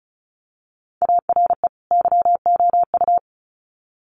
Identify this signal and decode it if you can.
Morse code, heard as a keyed tone, about 34 wpm, 700 Hz. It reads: ARE YOU
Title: Sets of 2 Words 35wpm